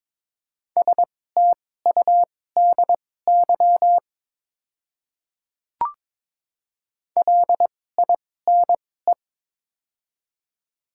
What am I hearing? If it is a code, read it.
Morse code, heard as a keyed tone, about 22 wpm, spaced as for 16 wpm, 690 Hz. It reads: STUDY LINE
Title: 2x Character Spacing - Top 300 Words - 22wpm